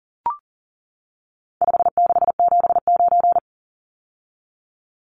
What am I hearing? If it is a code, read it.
Morse code, heard as a keyed tone, about 37 wpm, 700 Hz. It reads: 5679
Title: Numbers - 4-Digit 40wpm